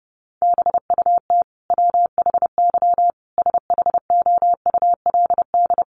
Morse code, heard as a keyed tone, about 30 wpm, 700 Hz. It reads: BUT WHY SHOULD
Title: Sentences from Top 1100 Words 30wpm